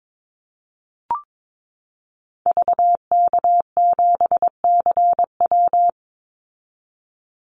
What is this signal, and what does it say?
Morse code, heard as a keyed tone, about 22 wpm, 690 Hz. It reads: VK7CW
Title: Call Signs Repeated 2x - 22wpm